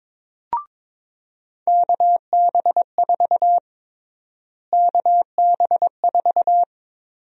Morse code, heard as a keyed tone, about 22 wpm, 700 Hz. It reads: KB4 KB4
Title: USA Call Sign Prefixes - Repeated 2x - 22wpm